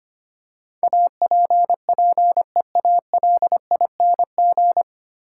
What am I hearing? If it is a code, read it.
Morse code, heard as a keyed tone, about 25 wpm, 700 Hz. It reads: APPEALING
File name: practice / 25wpm / 9 Letter Words - First 3 Letters Given - 25wpm